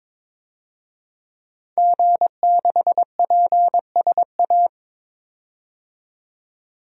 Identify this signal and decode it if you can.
Morse code, heard as a keyed tone, about 22 wpm, 700 Hz. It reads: G6PSA
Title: DXCC-100 Call Signs - 5 Characters - 22wpm